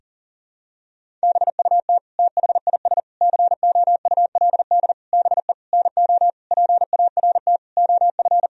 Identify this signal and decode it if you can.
Morse code, heard as a keyed tone, about 40 wpm, 700 Hz. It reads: BUT THIS COULD BE NO PART OF
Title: Sentences from Top 1000 Words - Review Part 2 - 40wpm